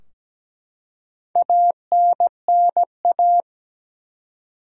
Morse code, heard as a keyed tone, about 17 wpm, 700 Hz. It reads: ANNA
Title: USA Common Names - Max 4 Letters - 17wpm